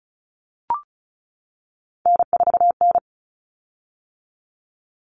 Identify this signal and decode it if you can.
Morse code, heard as a keyed tone, about 34 wpm, 700 Hz. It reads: N4N
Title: Common Cut Numbers - 3-Digit 35wpm